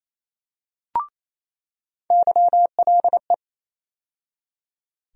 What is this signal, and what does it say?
Morse code, heard as a keyed tone, about 28 wpm, 700 Hz. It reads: YLE
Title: USA Call Sign Postfixes - 3 Letters 28wpm